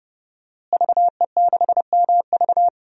Morse code, heard as a keyed tone, about 32 wpm, 700 Hz. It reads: VE6MV
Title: DXCC-100 Call Signs - 5 Characters - Repeated 3x - 30wpm